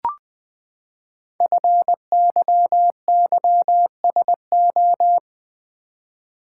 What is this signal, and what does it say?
Morse code, heard as a keyed tone, about 20 wpm, 700 Hz. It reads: FYYSO
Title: Code Groups - 5 Letters - 20wpm